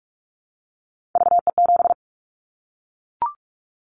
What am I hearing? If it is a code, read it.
Morse code, heard as a keyed tone, about 44 wpm, 700 Hz. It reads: VE7
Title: Callsign Prefixes for Top 100 DXCC - 45wpm